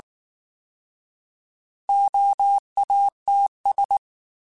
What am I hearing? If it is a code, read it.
Morse code, heard as a keyed tone, about 19 wpm, 800 Hz. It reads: OATS
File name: Alphabetical / Alphabetical mp3s / Os @18